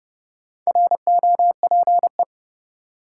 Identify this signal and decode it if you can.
Morse code, heard as a keyed tone, about 32 wpm, 700 Hz. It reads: ROPE
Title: Four Letter Words - 30wpm